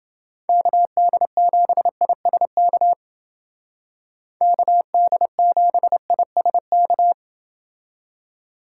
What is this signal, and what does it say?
Morse code, heard as a keyed tone, about 29 wpm, 700 Hz. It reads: KD7ISK KD7ISK